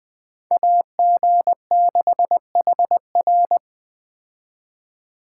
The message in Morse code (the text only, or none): AG6HR